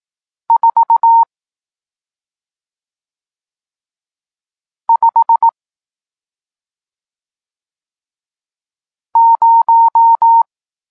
Morse code, heard as a keyed tone, about 18 wpm, 920 Hz. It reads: 4 5 0
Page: Tekst 40 – Nauka znaków przestankowych